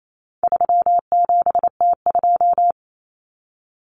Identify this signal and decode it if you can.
Morse code, heard as a keyed tone, about 28 wpm, 700 Hz. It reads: 37T2
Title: Common Cut Numbers - 4-Digit 28wpm